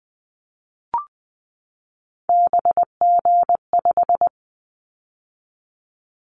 Morse code, heard as a keyed tone, about 19 wpm, 700 Hz. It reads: BG5
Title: Callsign Prefixes for Top 100 DXCC - 20wpm